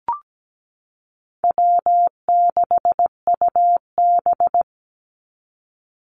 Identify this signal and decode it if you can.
Morse code, heard as a keyed tone, about 17 wpm, 700 Hz. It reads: W6UB
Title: NA SOTA Chaser Callsigns 17wpm